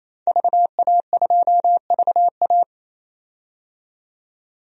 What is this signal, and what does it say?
Morse code, heard as a keyed tone, about 28 wpm, 700 Hz. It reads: VA2VA
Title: DXCC-100 Call Signs - 5 Characters - 28wpm